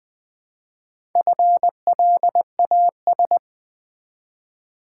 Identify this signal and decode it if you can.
Morse code, heard as a keyed tone, about 20 wpm, 700 Hz. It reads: FLAS
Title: Build A Word - 1 Letter At A Time - 5-Letter Words - 20wpm